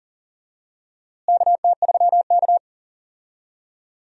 Morse code, heard as a keyed tone, about 40 wpm, 700 Hz. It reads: KT3K
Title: USA Call Signs - 4 Characters - Repeated 2x - 40wpm